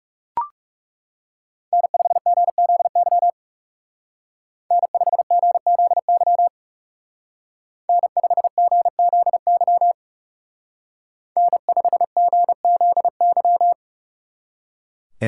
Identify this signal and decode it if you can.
Morse code, heard as a keyed tone, about 37 wpm, 700 Hz. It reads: N5GZY N5GZY N5GZY N5GZY
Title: Speed Racing - USA 1x3 Call Signs 30wpm